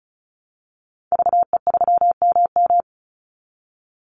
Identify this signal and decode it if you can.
Morse code, heard as a keyed tone, about 34 wpm, 700 Hz. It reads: VE3MM
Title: Call Signs Repeated 2x - 35wpm